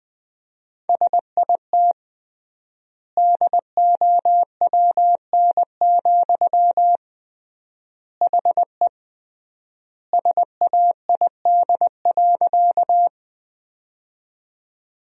Morse code, heard as a keyed tone, about 20 wpm, 700 Hz. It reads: SIT DOWN, HE SAID.
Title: 3x Word-Spacing - Sentences from Top 500 Words - 20wpm